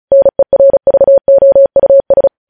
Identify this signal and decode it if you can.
Morse code, heard as a keyed tone, about 36 wpm, 550 Hz. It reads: NERVOUS
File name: podcast / News headlines-1x-35x35